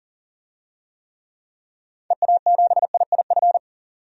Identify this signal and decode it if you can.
Morse code, heard as a keyed tone, about 37 wpm, 700 Hz. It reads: EA7IIF
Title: DXCC-100 Call Signs - 6 Characters - 40wpm